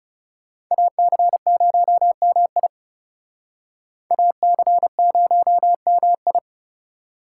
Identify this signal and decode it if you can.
Morse code, heard as a keyed tone, about 39 wpm, 700 Hz. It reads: AC0MI AC0MI